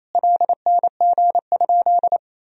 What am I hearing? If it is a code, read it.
Morse code, heard as a keyed tone, about 28 wpm, 700 Hz. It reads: LNG?
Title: Sentences from Top 500 Words and Abbreviations 28wpm